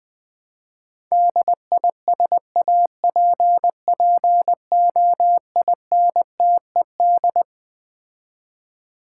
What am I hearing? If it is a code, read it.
Morse code, heard as a keyed tone, about 20 wpm, 700 Hz. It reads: DISAPPOINTED
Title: Build A Word - 6 Letters At A Time - 12-Letter Words - 20wpm